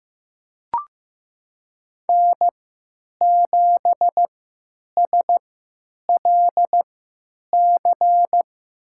Morse code, heard as a keyed tone, about 15 wpm, spaced as for 9 wpm, 700 Hz. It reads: N7SLC
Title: USA Call Signs - 5 Characters - 3x Character Spacing - 15wpm